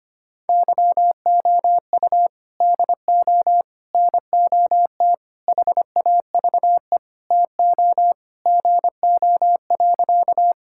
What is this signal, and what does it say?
Morse code, heard as a keyed tone, about 25 wpm, 700 Hz. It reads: YOU DO NOT HAVE TO GO.
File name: practice / 25wpm / Sentences with 4-Word Phrases - 25wpm